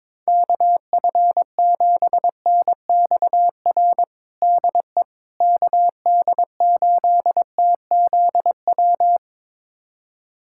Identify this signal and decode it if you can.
Morse code, heard as a keyed tone, about 22 wpm, 700 Hz. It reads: KF7NXR DE KD8TZW
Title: USA Callsign DE Callsign - 6 Characters - 22wpm